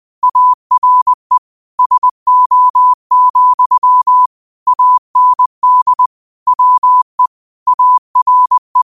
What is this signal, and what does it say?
Morse code, heard as a keyed tone, about 20 wpm, 1000 Hz. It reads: ARE SO, AND WE ARE
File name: 20wpm